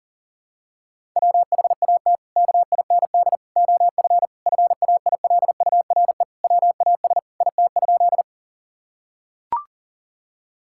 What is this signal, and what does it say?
Morse code, heard as a keyed tone, about 40 wpm, 700 Hz. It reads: WHAT KIND OF FAILURE WAS IT?
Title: Sentences from Top 1000 Words - Review Part 6 - 40wpm